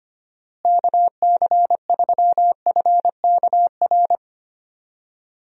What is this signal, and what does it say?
Morse code, heard as a keyed tone, about 25 wpm, 700 Hz. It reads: KC3FKR
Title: USA Call Signs - 6 Characters - Repeated 2x - 25wpm